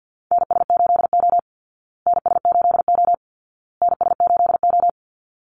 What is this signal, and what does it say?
Morse code, heard as a keyed tone, about 50 wpm, 700 Hz. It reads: NS8O NS8O NS8O
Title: Call Signs Repeated 2x - 50wpm